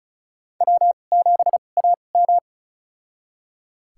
Morse code, heard as a keyed tone, about 36 wpm, spaced as for 25 wpm, 700 Hz. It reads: W7AM
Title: USA Call Signs - 4 Characters - 2x Character Spacing - 35wpm